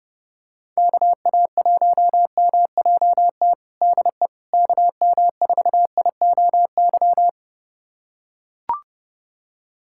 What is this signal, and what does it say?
Morse code, heard as a keyed tone, about 32 wpm, 700 Hz. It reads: KA1MJT DE KM4IOY
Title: USA Callsign DE Callsign - 6 Characters - 30wpm